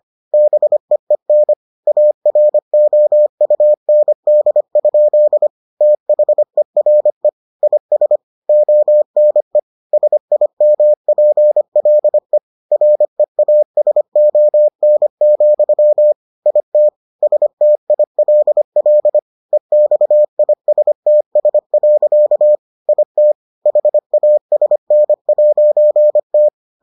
Morse code, heard as a keyed tone, about 25 wpm, 600 Hz. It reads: BEEN AROUND? THERE IS ONE SIMPLE REASON, IT STILL EXISTS. IT HASN'T